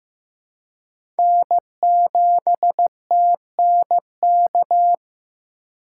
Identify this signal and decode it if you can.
Morse code, heard as a keyed tone, about 16 wpm, 700 Hz. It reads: N7TNK
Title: USA Call Signs - 2x Character Spacing - 15wpm